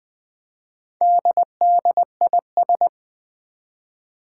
Morse code, heard as a keyed tone, about 20 wpm, 700 Hz. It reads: DDIS
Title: Four Challenging Letter-Number Combinations 20wpm